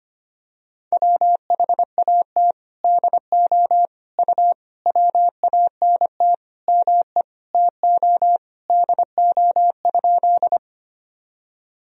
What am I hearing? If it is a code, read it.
Morse code, heard as a keyed tone, about 25 wpm, 700 Hz. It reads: WHAT DO U WANT ME TO DO?
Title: Sentences from Top 200 Words and Abbreviations 25wpm